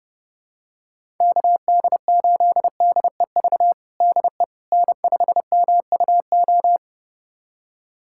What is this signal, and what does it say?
Morse code, heard as a keyed tone, about 30 wpm, 700 Hz. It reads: KD8DEV DE N5MUO
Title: USA Callsign DE Callsign 30wpm